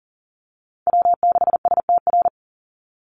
Morse code, heard as a keyed tone, about 40 wpm, 700 Hz. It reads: W6STR
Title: NA SOTA Chaser Callsigns 40wpm